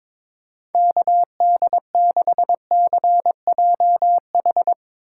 Morse code, heard as a keyed tone, about 23 wpm, 700 Hz. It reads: KD6CJH
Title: USA Call Signs - 6 Characters - 2x Character Spacing - 22wpm